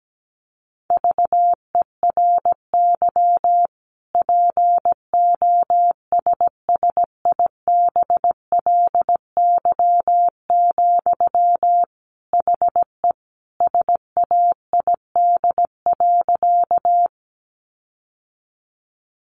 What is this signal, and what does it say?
Morse code, heard as a keyed tone, about 17 wpm, 700 Hz. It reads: VERY POSSIBLY, HE SAID.
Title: Sentences from Top 1300 Words 17wpm